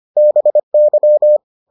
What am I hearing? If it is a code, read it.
Morse code, heard as a keyed tone, about 25 wpm, 600 Hz. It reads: BY